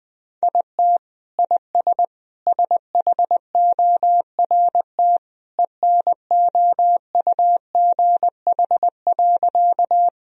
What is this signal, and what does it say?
Morse code, heard as a keyed tone, about 20 wpm, 700 Hz. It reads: IT IS SHORT ENOUGH.
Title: Sentences from Top 1000 Words - Review Part 2 - 20wpm